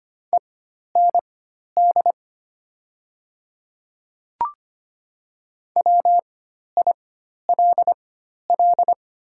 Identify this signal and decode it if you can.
Morse code, heard as a keyed tone, about 25 wpm, spaced as for 12 wpm, 700 Hz. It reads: END WILL
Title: 4x Character Spacing - Top 200 Words - 25wpm